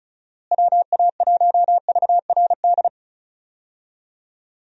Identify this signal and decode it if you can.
Morse code, heard as a keyed tone, about 34 wpm, 700 Hz. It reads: WA1VRD
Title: USA Call Signs - 6 Characters - Repeated 3x - 35wpm